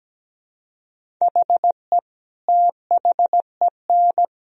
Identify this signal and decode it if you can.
Morse code, heard as a keyed tone, about 17 wpm, 700 Hz. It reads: HE THEN
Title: Sentences from Top 300 Words and Abbreviations 17wpm